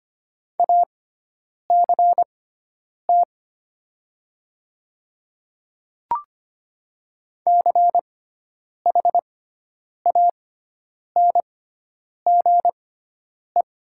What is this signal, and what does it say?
Morse code, heard as a keyed tone, about 25 wpm, spaced as for 9 wpm, 700 Hz. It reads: ACT CHANGE
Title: 6x Character Spacing - Top 200 Words - 25wpm